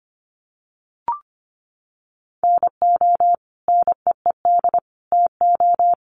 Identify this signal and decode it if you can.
Morse code, heard as a keyed tone, about 25 wpm, 700 Hz. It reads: NO NEED TO
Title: Sentences from Top 1100 Words 25wpm